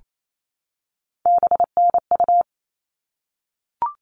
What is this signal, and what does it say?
Morse code, heard as a keyed tone, about 28 wpm, 700 Hz. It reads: BNU
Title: All Cut Numbers - 3-Digit 28wpm